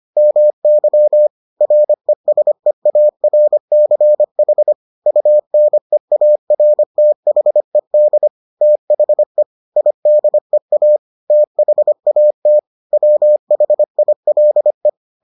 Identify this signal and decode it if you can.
Morse code, heard as a keyed tone, about 25 wpm, 600 Hz. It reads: MY RESEARCH UNEARTHED THE IDEA THAT WHILE